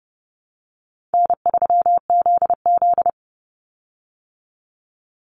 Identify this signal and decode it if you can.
Morse code, heard as a keyed tone, about 30 wpm, 700 Hz. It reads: N3ZZ
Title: Call Signs Repeated 2x - 30wpm